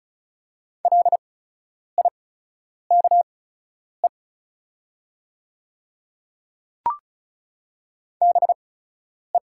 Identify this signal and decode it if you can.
Morse code, heard as a keyed tone, about 36 wpm, spaced as for 10 wpm, 700 Hz. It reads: LIKE BE
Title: 8x Character Spacing - Top 300 Words - 35wpm